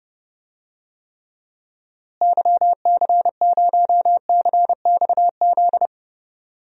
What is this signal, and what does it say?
Morse code, heard as a keyed tone, about 32 wpm, 700 Hz. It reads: YC0CXZ
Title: DXCC-100 Call Signs - 6 Characters - Repeated 2x - 30wpm